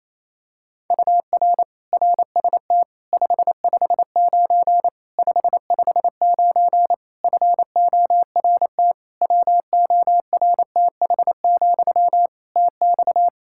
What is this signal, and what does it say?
Morse code, heard as a keyed tone, about 27 wpm, 700 Hz. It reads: UR RST 559 559 FORT WORTH, TX
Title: Practice QSO - Exchange 1 - Beginning - 28wpm